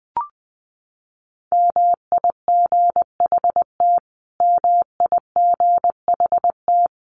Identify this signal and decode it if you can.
Morse code, heard as a keyed tone, about 20 wpm, 700 Hz. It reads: MIGHT MIGHT
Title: Top 500 Words Repeated 20wpm